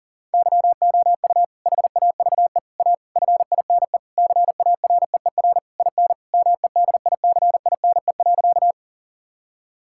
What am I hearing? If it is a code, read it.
Morse code, heard as a keyed tone, about 40 wpm, 700 Hz. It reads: YOU HAVE A FINE CAREER IN MEDICINE.
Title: Sentences from Top 1000 Words - Review Part 6 - 40wpm